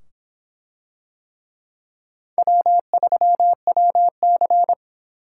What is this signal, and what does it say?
Morse code, heard as a keyed tone, about 25 wpm, 700 Hz. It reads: W3WC